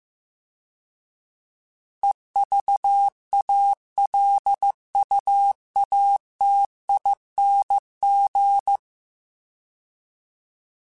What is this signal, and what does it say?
Morse code, heard as a keyed tone, about 15 wpm, 800 Hz. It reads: EVALUATING
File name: Alphabetical / Alphabetical mp3s / Es @13